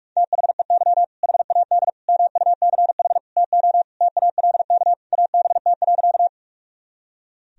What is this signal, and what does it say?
Morse code, heard as a keyed tone, about 46 wpm, 700 Hz. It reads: THEY HAD MUCH TO TALK ABT.
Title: Sentences from Top 400 Words and Abbreviations 45wpm